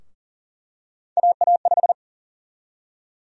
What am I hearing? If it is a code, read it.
Morse code, heard as a keyed tone, about 40 wpm, 700 Hz. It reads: AA5
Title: USA Call Sign Prefixes - Repeated 2x - 40wpm